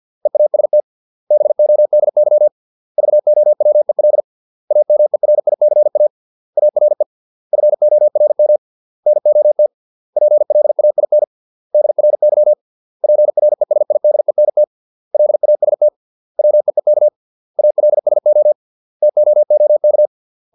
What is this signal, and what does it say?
Morse code, heard as a keyed tone, about 50 wpm, 600 Hz. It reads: EAST BODY VOWEL AMERICA ARE FORM NOT PLAIN DRY PRESIDENT LAST WEEK ALSO TOOK